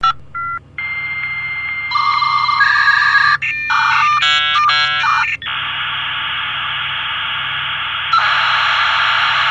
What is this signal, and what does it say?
Dial-up modem handshake.